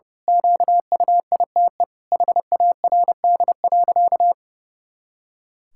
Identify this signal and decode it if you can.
Morse code, heard as a keyed tone, about 30 wpm, 700 Hz. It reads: QUITE HARD.
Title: Pangram Challenge v2 - 30wpm